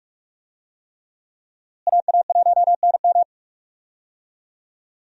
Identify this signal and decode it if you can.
Morse code, heard as a keyed tone, about 46 wpm, 700 Hz. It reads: AA1NM
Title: USA Call Signs - 5 Characters - Repeated 3x - 45wpm